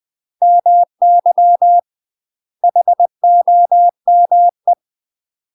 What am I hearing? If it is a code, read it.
Morse code, heard as a keyed tone, about 20 wpm, 700 Hz. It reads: MY HOME
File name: UK MASTER QSOs TRAINING / 20WPM / UK MASTER QSOs 0047